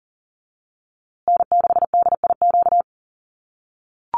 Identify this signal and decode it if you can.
Morse code, heard as a keyed tone, about 40 wpm, 700 Hz. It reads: N6DIQ
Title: USA Call Signs Repeated 3x 40wpm